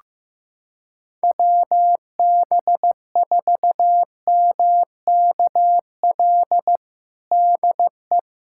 Morse code, heard as a keyed tone, about 15 wpm, 700 Hz. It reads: WB4MKL DE
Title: USA Callsign DE Callsign - 6 Characters - 15wpm